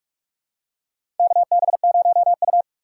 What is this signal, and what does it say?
Morse code, heard as a keyed tone, about 45 wpm, 700 Hz. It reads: KB0U
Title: USA Call Signs - 4 Characters - 2x Character Spacing - 45wpm